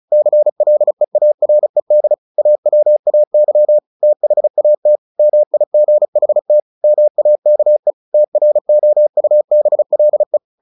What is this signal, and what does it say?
Morse code, heard as a keyed tone, about 35 wpm, 600 Hz. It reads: CLEARED AWAY THAT MIGHT MAKE TROUBLE
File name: the old man and the sea / the old man and the sea 1 35wpm